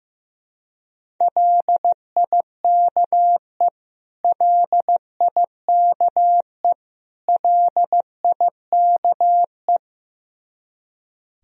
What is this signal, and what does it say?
Morse code, heard as a keyed tone, about 16 wpm, 700 Hz. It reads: LIKE LIKE LIKE
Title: Top 100 Words Repeated 15wpm